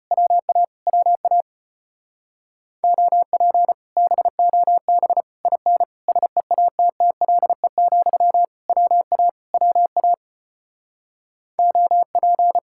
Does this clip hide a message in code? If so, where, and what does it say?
Yes: Morse code from 0.1 to 12.6 s, heard as a keyed tone, about 33 wpm, 700 Hz. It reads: WA WA OP BOB IN SEATTLE, WA WA OP